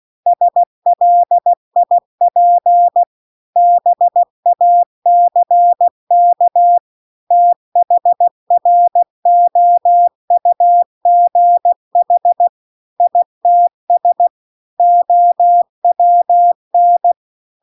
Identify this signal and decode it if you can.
Morse code, heard as a keyed tone, about 16 wpm, 700 Hz. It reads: SLIP BACK THROUGH ITS OWN